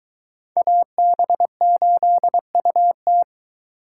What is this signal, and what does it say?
Morse code, heard as a keyed tone, about 22 wpm, 700 Hz. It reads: AB8UT